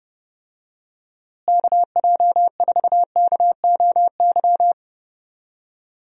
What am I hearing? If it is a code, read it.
Morse code, heard as a keyed tone, about 30 wpm, 700 Hz. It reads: KJ4KOY
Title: USA Call Signs - 6 Characters - Repeated 3x - 30wpm